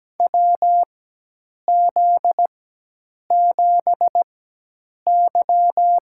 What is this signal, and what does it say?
Morse code, heard as a keyed tone, about 17 wpm, spaced as for 8 wpm, 700 Hz. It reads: WZ7Y
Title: USA Call Signs - 4x Character Spacing - 17wpm